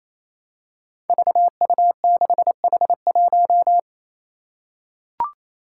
Morse code, heard as a keyed tone, about 28 wpm, 700 Hz. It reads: VU6H1
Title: Five Challenging Letter-Number Combinations 28wpm